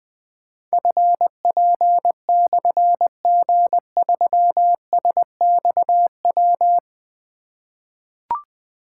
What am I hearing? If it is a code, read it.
Morse code, heard as a keyed tone, about 20 wpm, 700 Hz. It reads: FP/G3SXW
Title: Difficult Call Signs 20wpm